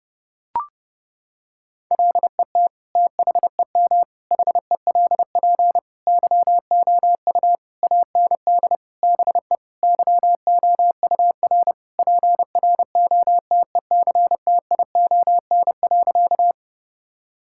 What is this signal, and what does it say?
Morse code, heard as a keyed tone, about 30 wpm, 700 Hz. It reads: LET THEM HELP YOU AND BE YOUR PROTECTION.
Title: Sentences from Top 1600 Words 30wpm